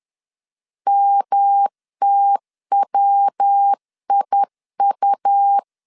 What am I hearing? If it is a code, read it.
Morse code, heard as a keyed tone, about 10 wpm, 800 Hz. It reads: MTWIU
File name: CW / DET / 1 3 1500C S B 10wpm 3335